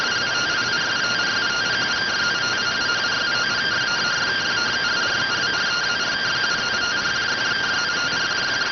Two-tone teleprinter signal, about 170 Hz shift.